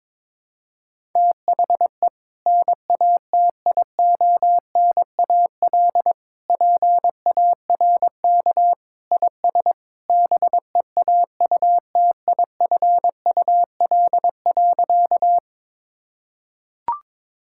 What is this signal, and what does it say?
Morse code, heard as a keyed tone, about 22 wpm, 700 Hz. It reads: THE NATIONAL PARK IS BEAUTIFUL.
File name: practice / 22wpm / Sentences with 2-Word Phrases - 22wpm